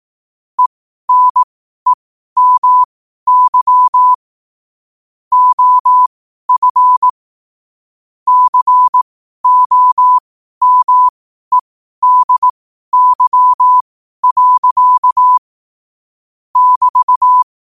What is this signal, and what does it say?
Morse code, heard as a keyed tone, about 18 wpm, spaced as for 12 wpm, 1000 Hz. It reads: ENEMY OF COMEDY. <BT>
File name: qotdfiles / 12wpm